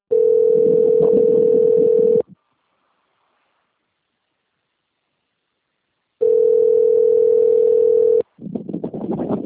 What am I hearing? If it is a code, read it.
Ringback tone (North America), 440 + 480 Hz.
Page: Dantas - PE